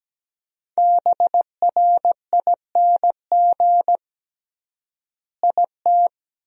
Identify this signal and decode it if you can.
Morse code, heard as a keyed tone, about 17 wpm, 700 Hz. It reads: BRING IT
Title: 3x Word-Spacing - Sentences from Top 500 Words - 17wpm